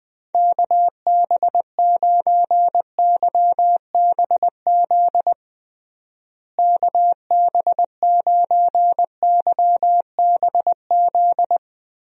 Morse code, heard as a keyed tone, about 20 wpm, 700 Hz. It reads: KB9YBZ KB9YBZ
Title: USA Call Signs - 6 Characters - Repeated 3x - 20wpm